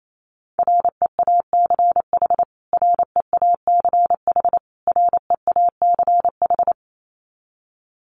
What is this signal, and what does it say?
Morse code, heard as a keyed tone, about 28 wpm, 700 Hz. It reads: REACH REACH REACH
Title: Top 500 Words Repeated 28wpm